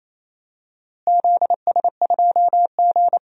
Morse code, heard as a keyed tone, about 28 wpm, 700 Hz. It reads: ZS2G
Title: DXCC-100 Call Signs - 4 Characters - Repeated 2x - 28wpm